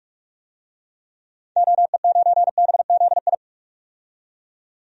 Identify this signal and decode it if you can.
Morse code, heard as a keyed tone, about 41 wpm, 700 Hz. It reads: OE9BZI
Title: DXCC-100 Call Signs - 6 Characters - 45wpm